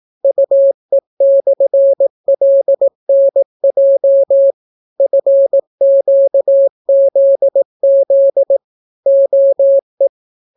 Morse code, heard as a keyed tone, about 18 wpm, 550 Hz. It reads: UE/LNJ FQZZ OE